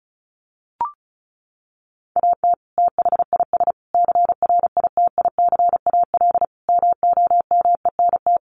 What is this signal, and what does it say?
Morse code, heard as a keyed tone, about 34 wpm, 700 Hz. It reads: AT THIS CRITICAL MOMENT
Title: Sentences from Top 1100 Words 35wpm